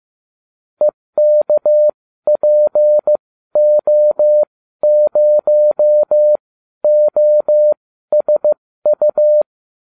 Morse code, heard as a keyed tone, about 15 wpm, spaced as for 12 wpm, 600 Hz. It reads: EKPO0OSU